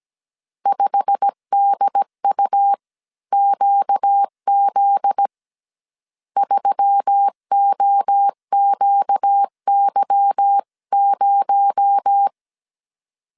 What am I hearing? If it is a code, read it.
Morse code, heard as a keyed tone, about 17 wpm, 800 Hz. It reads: 5DUQZ 3OQY0
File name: CW / DET / 1 9 1500C S C 16wpm 2909